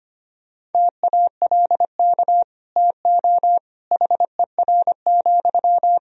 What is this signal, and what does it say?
Morse code, heard as a keyed tone, about 25 wpm, 700 Hz. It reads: TALK TO HER,
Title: Sentences from Top 1000 Words - Review Part 3 - 25wpm